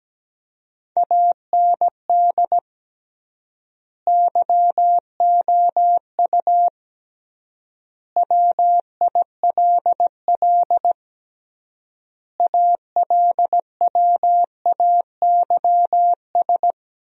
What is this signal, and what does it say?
Morse code, heard as a keyed tone, about 17 wpm, 700 Hz. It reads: AND YOU WILL ALWAYS
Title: 3x Word-Spacing - Sentences from Top 400 Words - 17wpm